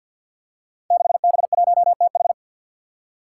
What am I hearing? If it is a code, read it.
Morse code, heard as a keyed tone, about 53 wpm, 700 Hz. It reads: 6B1TH
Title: Five Challenging Letter-Number Combinations 50wpm